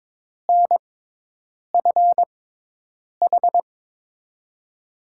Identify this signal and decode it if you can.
Morse code, heard as a keyed tone, about 21 wpm, spaced as for 8 wpm, 700 Hz. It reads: NFH
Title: USA Call Signs - 5 Characters - 6x Character Spacing - 22wpm